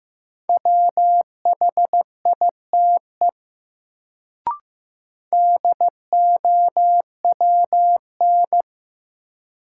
Morse code, heard as a keyed tone, about 15 wpm, 700 Hz. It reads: WHITE DOWN
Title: Sentences from Top 300 Words - New Word Drill - 15wpm